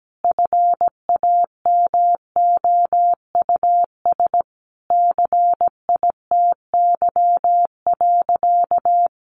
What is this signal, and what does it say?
Morse code, heard as a keyed tone, about 17 wpm, 700 Hz. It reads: FAMOUS CITY.
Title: Sentences from Top 1500 Words 17wpm